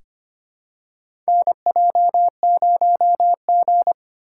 Morse code, heard as a keyed tone, about 25 wpm, 700 Hz. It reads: NJ0G
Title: USA Call Signs - 4 Characters - 2x Character Spacing - 25wpm